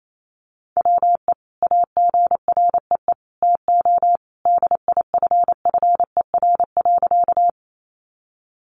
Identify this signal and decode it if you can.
Morse code, heard as a keyed tone, about 28 wpm, 700 Hz. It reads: WE AGREE TO DIFFER.
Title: Sentences from Top 1100 Words 28wpm